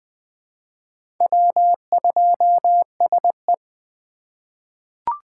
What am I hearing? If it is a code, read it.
Morse code, heard as a keyed tone, about 20 wpm, 700 Hz. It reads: W2SE
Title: 2023 SOTA Chaser Call Signs - 20wpm